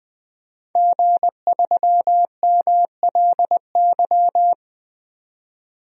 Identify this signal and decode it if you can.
Morse code, heard as a keyed tone, about 20 wpm, 700 Hz. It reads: G3MLY
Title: DXCC-100 Call Signs - 5 Characters - 20wpm